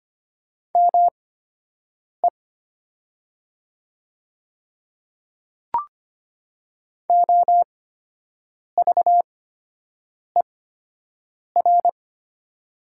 Morse code, heard as a keyed tone, about 25 wpm, spaced as for 7 wpm, 700 Hz. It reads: ME OVER
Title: 8x Character Spacing - Top 300 Words - 25wpm